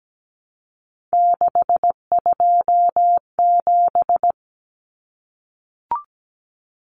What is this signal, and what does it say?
Morse code, heard as a keyed tone, about 17 wpm, 700 Hz. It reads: 627
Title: Common Cut Numbers - 3-Digit 17wpm